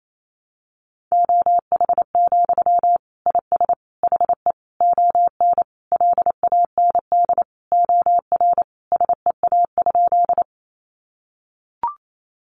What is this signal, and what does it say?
Morse code, heard as a keyed tone, about 28 wpm, 700 Hz. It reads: OH, IS HE ON LAND OR SEA?
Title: Sentences from Top 500 Words 28wpm